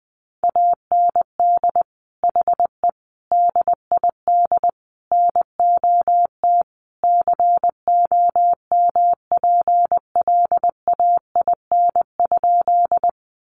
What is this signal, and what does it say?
Morse code, heard as a keyed tone, about 19 wpm, 700 Hz. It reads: AND HE DID NOT COMPLAIN?
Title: Sentences from Top 1300 Words 20wpm